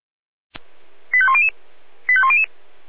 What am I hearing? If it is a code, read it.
Five-tone selective call (ZVEI), heard as quick runs of five tones, tones of about 70 ms. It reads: 85100 85100